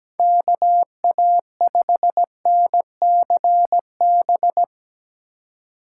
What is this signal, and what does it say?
Morse code, heard as a keyed tone, about 17 wpm, 700 Hz. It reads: KA5NCB
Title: USA Call Signs - 6 Characters - 4x Character Spacing - 17wpm